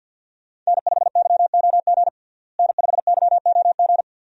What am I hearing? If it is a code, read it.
Morse code, heard as a keyed tone, about 52 wpm, 700 Hz. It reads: N5YOG N5YOG
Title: USA Call Signs - 5 Characters - Repeated 2x - 50wpm